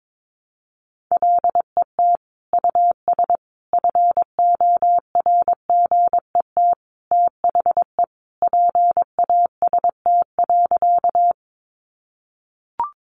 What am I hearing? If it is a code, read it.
Morse code, heard as a keyed tone, about 22 wpm, 700 Hz. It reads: LET US FORGET THE PAST.
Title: Sentences from Top 1200 Words 22wpm